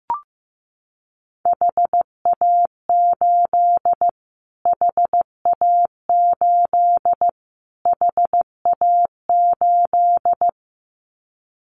Morse code, heard as a keyed tone, about 15 wpm, 700 Hz. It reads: HA8 HA8 HA8
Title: Callsign Prefixes for Top 25 DXCC - Repeated 3x - 15wpm